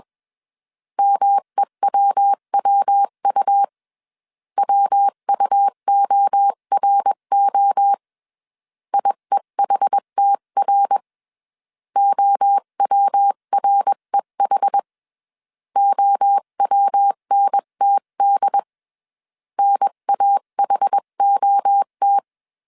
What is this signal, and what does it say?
Morse code, heard as a keyed tone, about 21 wpm, 800 Hz. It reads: MEWWU WUORO IEHTR OWREH OWNTD NAHOT
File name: 20wpm